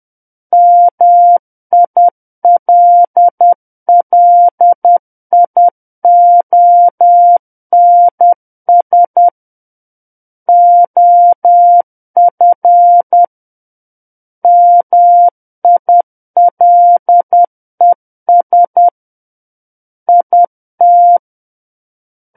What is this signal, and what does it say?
Morse code, heard as a keyed tone, about 10 wpm, 700 Hz. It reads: MILLIONS OF MILES IT
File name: Book One / 01-Chapter 1 - 10 WPM